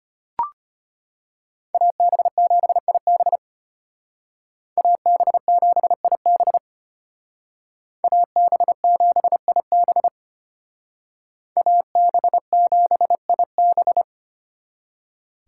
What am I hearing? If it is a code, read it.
Morse code, heard as a keyed tone, about 32 wpm, 700 Hz. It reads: AB7IB AB7IB AB7IB AB7IB